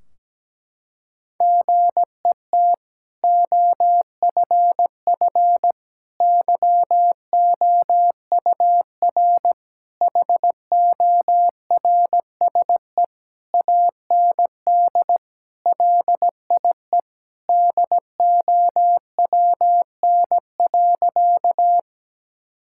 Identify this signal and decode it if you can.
Morse code, heard as a keyed tone, about 17 wpm, 700 Hz. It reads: GET OFF YOUR HORSE AND LIE DOWN.
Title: Sentences from Top 1600 Words 17wpm